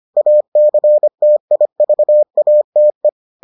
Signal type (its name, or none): Morse code